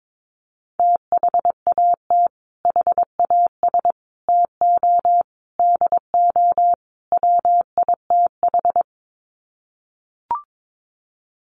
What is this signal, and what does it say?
Morse code, heard as a keyed tone, about 22 wpm, 700 Hz. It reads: THAT HAS TO DO WITH
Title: Sets of 5 Words Encore2 22wpm